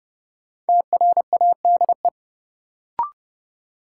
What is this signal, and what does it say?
Morse code, heard as a keyed tone, about 31 wpm, 700 Hz. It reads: TRADE
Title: Sets of 5 Words Encore - New Word Drill - 30wpm